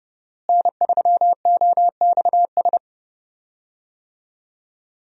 Morse code, heard as a keyed tone, about 30 wpm, 700 Hz. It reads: N3OXS
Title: USA Call Signs - 5 Characters - Repeated 2x - 30wpm